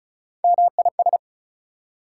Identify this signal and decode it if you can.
Morse code, heard as a keyed tone, about 36 wpm, 700 Hz. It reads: MIS
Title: Prefixes and Suffixes 35wpm